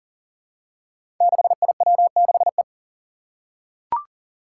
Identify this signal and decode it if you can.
Morse code, heard as a keyed tone, about 40 wpm, 700 Hz. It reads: 6IW6E
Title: Five Challenging Letter-Number Combinations 40wpm